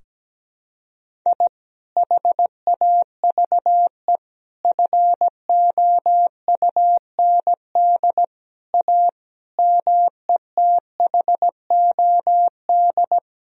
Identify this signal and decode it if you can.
Morse code, heard as a keyed tone, about 17 wpm, 700 Hz. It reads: I HAVE FOUND A METHOD
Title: Sentences from Top 1000 Words - Review Part 3 - 17wpm